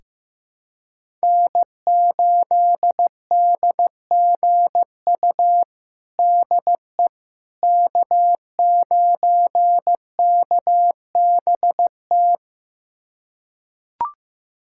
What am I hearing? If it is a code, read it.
Morse code, heard as a keyed tone, about 16 wpm, 700 Hz. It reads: N8DGU DE K9KBT
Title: USA Callsign DE Callsign - 5 Characters - 15wpm